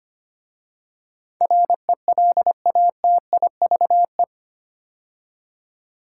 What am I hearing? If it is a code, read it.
Morse code, heard as a keyed tone, about 25 wpm, 700 Hz. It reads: RELATIVE
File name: 8 Letter Words - First 3 Letters Given - 25wpm